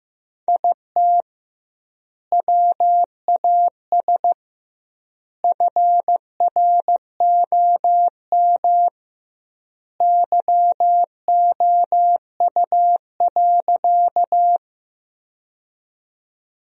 Morse code, heard as a keyed tone, about 15 wpm, 700 Hz. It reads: IT WAS FROM YOU.
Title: 2x Word-Spacing - Sentences from Top 100 Words - 15wpm